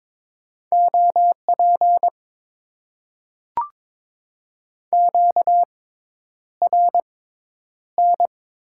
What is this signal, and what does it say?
Morse code, heard as a keyed tone, about 22 wpm, spaced as for 10 wpm, 700 Hz. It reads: OP QRN
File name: Top 90 QSO Elements - 6x Character Spacing - 22wpm